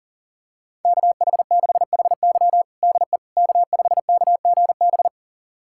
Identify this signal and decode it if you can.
Morse code, heard as a keyed tone, about 40 wpm, 700 Hz. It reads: KH6HY DE K5KGB
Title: USA Callsign DE Callsign - 5 Characters - 40wpm